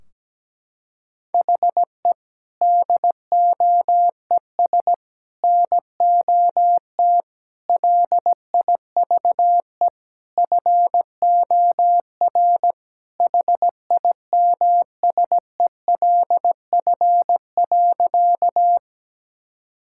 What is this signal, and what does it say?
Morse code, heard as a keyed tone, about 17 wpm, 700 Hz. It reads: HE DOES NOT LIVE FOR HIMSELF.
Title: Sentences from Top 1600 Words 17wpm